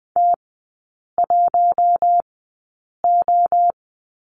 Morse code, heard as a keyed tone, about 20 wpm, spaced as for 8 wpm, 700 Hz. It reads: T1O
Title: 060 Lesson 11e - Letter h - Three Characters 20wpm